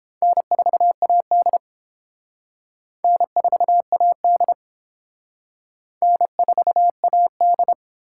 Morse code, heard as a keyed tone, about 29 wpm, 700 Hz. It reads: N4AD N4AD N4AD